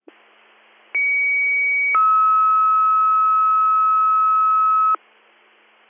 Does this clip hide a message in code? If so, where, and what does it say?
Yes: two-tone page from 0.9 to 5.0 s, heard as a one-second tone, then a three-second tone, about 2250 then 1290 Hz.